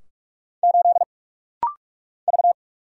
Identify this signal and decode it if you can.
Morse code, heard as a keyed tone, about 44 wpm, 700 Hz. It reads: 8 V
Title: Single Letter-Number Alt - Rapid-Fire 45wpm